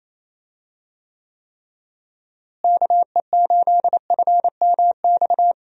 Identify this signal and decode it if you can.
Morse code, heard as a keyed tone, about 28 wpm, 700 Hz. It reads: KE8FMX
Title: USA Call Signs - 6 Characters - Repeated 3x - 28wpm